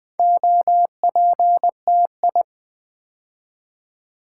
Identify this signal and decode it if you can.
Morse code, heard as a keyed tone, about 20 wpm, 700 Hz. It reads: OPTI
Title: Build A Word - 2 Letters At A Time - 6-Letter Words - 20wpm